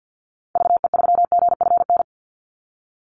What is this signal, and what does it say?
Morse code, heard as a keyed tone, about 50 wpm, 700 Hz. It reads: VE3GFN
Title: Call Signs - 50wpm